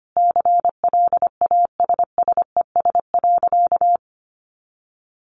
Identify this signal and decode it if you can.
Morse code, heard as a keyed tone, about 25 wpm, 700 Hz. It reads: CLASSES.
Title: Additional Easy Phrases 25wpm